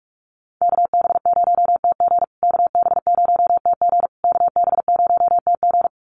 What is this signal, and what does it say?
Morse code, heard as a keyed tone, about 44 wpm, 700 Hz. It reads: KB0TG KB0TG KB0TG
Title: USA Call Signs Repeated 3x 45wpm